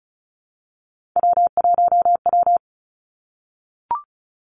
Morse code, heard as a keyed tone, about 36 wpm, 700 Hz. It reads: W1W
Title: Three Challenging Letter-Number Combinations 35wpm